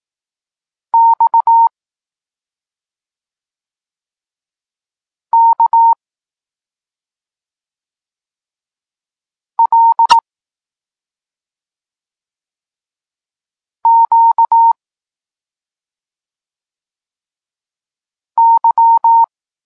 Morse code, heard as a keyed tone, about 18 wpm, 920 Hz. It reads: X K L Q Y
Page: Tekst 32 – Doskonalenie odbioru poznanych liter